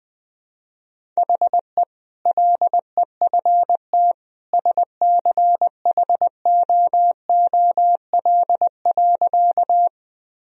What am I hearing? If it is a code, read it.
Morse code, heard as a keyed tone, about 20 wpm, 700 Hz. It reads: HE LEFT SCHOOL.
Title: Sentences from Top 1000 Words - Review Part 1 - 20wpm